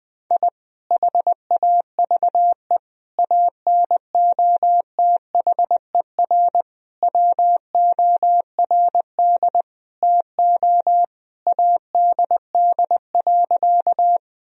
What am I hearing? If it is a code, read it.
Morse code, heard as a keyed tone, about 20 wpm, 700 Hz. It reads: I HAVE ANOTHER WORD TO ADD.
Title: Sentences from Top 1000 Words - Review Part 1 - 20wpm